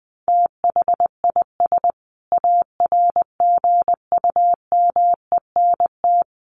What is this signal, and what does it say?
Morse code, heard as a keyed tone, about 20 wpm, 700 Hz. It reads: THIS ARGUMENT
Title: Sentences from Top 1400 Words 20wpm